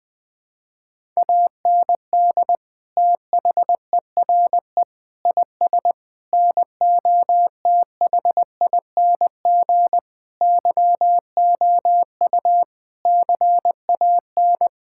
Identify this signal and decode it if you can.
Morse code, heard as a keyed tone, about 20 wpm, 700 Hz. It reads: AND THERE IS NOTHING YOU CAN
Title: Sentences with 8-Word Phrases - 20wpm